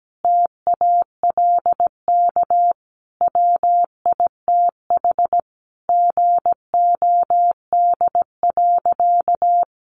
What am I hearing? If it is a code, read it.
Morse code, heard as a keyed tone, about 17 wpm, 700 Hz. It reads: TALK WITH GOD.
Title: A Sound Of Thunder 17wpm